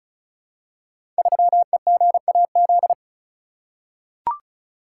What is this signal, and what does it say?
Morse code, heard as a keyed tone, about 34 wpm, 700 Hz. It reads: 3EGAZ
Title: Code Groups - 5 Letter:Numbers - 35wpm